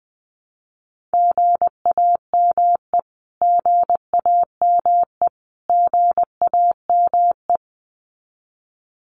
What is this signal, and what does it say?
Morse code, heard as a keyed tone, about 20 wpm, 700 Hz. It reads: GAME GAME GAME
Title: Top 500 Words Repeated 20wpm